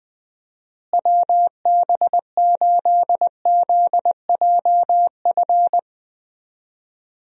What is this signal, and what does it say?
Morse code, heard as a keyed tone, about 20 wpm, 700 Hz. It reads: WB8ZJF
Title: USA Call Signs - 6 Characters - Repeated 3x - 20wpm